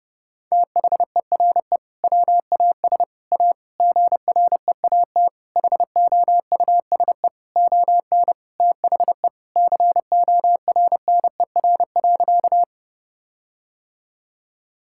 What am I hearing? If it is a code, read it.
Morse code, heard as a keyed tone, about 32 wpm, 700 Hz. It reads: THERE WAS A GREAT HOUSE ON THE CORNER.
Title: Sentences from Top 1600 Words 30wpm